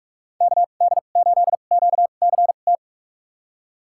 Morse code, heard as a keyed tone, about 46 wpm, spaced as for 33 wpm, 700 Hz. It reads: KD8QCT
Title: USA Call Signs - 6 Characters - 2x Character Spacing - 45wpm